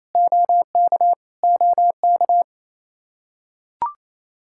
Morse code, heard as a keyed tone, about 28 wpm, 700 Hz. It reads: OK OK
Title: Practice QSO - Exchange 1 - Beginning - 28wpm